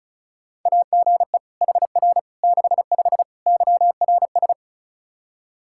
Morse code, heard as a keyed tone, about 34 wpm, 700 Hz. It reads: AGE HR 65 YRS
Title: Practice QSO - Exchange 3 - More Info - 35wpm